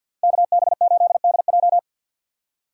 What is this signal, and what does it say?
Morse code, heard as a keyed tone, about 61 wpm, 700 Hz. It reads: KB8DJ
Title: Speed Racing - USA 2x2 Call Signs 50wpm